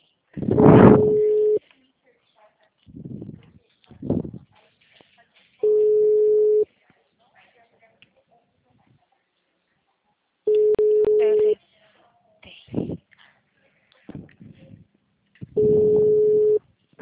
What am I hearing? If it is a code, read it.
Ringback tone (Europe, 425 Hz), with speech.